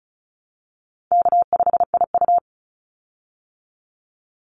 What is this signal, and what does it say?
Morse code, heard as a keyed tone, about 34 wpm, 700 Hz. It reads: K5IU
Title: Call Signs Repeated 2x - 35wpm